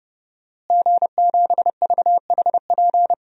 Morse code, heard as a keyed tone, about 30 wpm, 700 Hz. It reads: G7VHP
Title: DXCC-100 Call Signs - 5 Characters - Repeated 2x - 30wpm